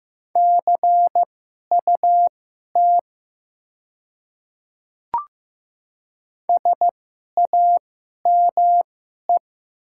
Morse code, heard as a keyed tone, about 15 wpm, spaced as for 11 wpm, 700 Hz. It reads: CUT SAME
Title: 2x Character Spacing - Top 300 Words - 15wpm